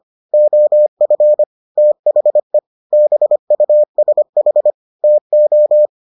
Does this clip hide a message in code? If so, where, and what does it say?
Yes: Morse code from 0.3 to 5.9 s, heard as a keyed tone, about 25 wpm, 600 Hz. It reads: OF THE BUSH TO